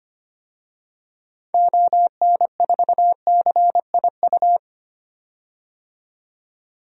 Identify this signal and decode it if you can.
Morse code, heard as a keyed tone, about 25 wpm, 700 Hz. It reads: ON4CIU
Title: DXCC-100 Call Signs - 6 Characters - 25wpm